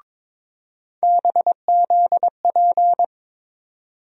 Morse code, heard as a keyed tone, about 22 wpm, 700 Hz. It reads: BZP
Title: USA Call Sign Postfixes - 3 Letters 22wpm